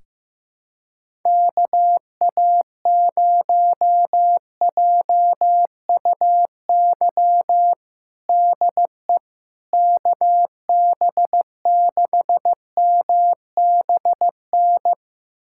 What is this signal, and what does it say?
Morse code, heard as a keyed tone, about 16 wpm, 700 Hz. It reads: KA0JUY DE KB6MBN
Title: USA Callsign DE Callsign 15wpm